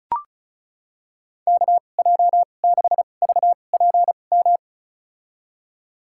Morse code, heard as a keyed tone, about 36 wpm, spaced as for 25 wpm, 700 Hz. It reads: KJ6VPM
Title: USA Call Signs - 2x Character Spacing - 35wpm